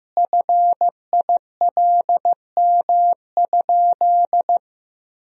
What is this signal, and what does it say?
Morse code, heard as a keyed tone, about 16 wpm, 700 Hz. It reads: FILM?
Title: Sentences from Top 1000 Words - Review Part 3 - 15wpm